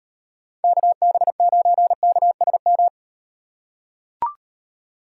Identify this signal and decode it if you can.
Morse code, heard as a keyed tone, about 38 wpm, 700 Hz. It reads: KB9KSM